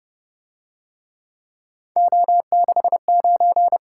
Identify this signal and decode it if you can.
Morse code, heard as a keyed tone, about 26 wpm, 700 Hz. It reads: O69